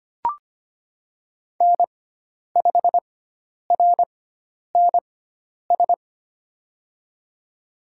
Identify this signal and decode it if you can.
Morse code, heard as a keyed tone, about 25 wpm, spaced as for 10 wpm, 700 Hz. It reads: N5RNS
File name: Lesson 16g - Letter c - Call Signs 25wpm